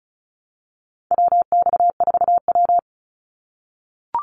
Morse code, heard as a keyed tone, about 34 wpm, 700 Hz. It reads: WX4W
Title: Call Signs - 35wpm